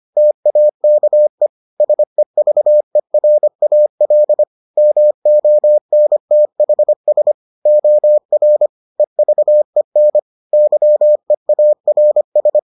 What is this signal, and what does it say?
Morse code, heard as a keyed tone, about 25 wpm, 600 Hz. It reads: TAKE SEVERAL MONTHS OR EVEN YEARS